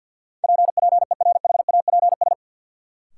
Morse code, heard as a keyed tone, about 50 wpm, 700 Hz. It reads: PPERHAPS
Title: Sentences from Top 1100 Words - New Word Drill - 50wpm